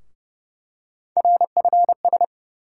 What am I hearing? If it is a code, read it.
Morse code, heard as a keyed tone, about 30 wpm, 700 Hz. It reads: RFS
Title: USA Call Sign Postfixes - 3 Letters 30wpm